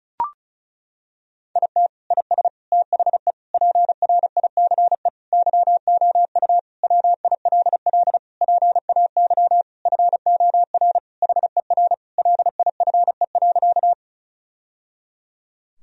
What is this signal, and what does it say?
Morse code, heard as a keyed tone, about 36 wpm, 700 Hz. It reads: IT IS THE PRICE YOU WILL PAY FOR HER LIFE.
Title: Sentences from Top 1000 Words - Review Part 4 - 35wpm